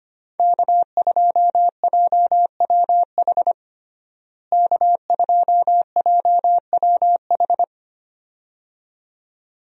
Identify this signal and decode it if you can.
Morse code, heard as a keyed tone, about 25 wpm, 700 Hz. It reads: K2JWH K2JWH
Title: USA Call Signs - 5 Characters - Repeated 2x - 25wpm